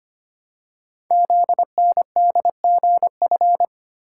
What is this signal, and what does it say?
Morse code, heard as a keyed tone, about 25 wpm, 700 Hz. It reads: ZNDGF
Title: Code Groups - 5 Letters - 25wpm